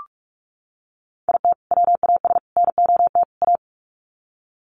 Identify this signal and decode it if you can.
Morse code, heard as a keyed tone, about 44 wpm, 700 Hz. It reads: IT WAS NOT A
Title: Sets of 4 Words 45wpm